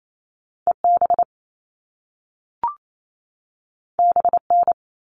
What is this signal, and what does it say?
Morse code, heard as a keyed tone, about 28 wpm, 700 Hz. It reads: EB BN
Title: All Cut Numbers - 2-Digit 28wpm